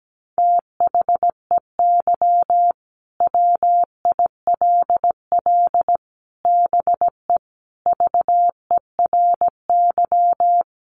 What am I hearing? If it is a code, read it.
Morse code, heard as a keyed tone, about 17 wpm, 700 Hz. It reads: THEY WILL BE VERY
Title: Sentences from Top 1100 Words 17wpm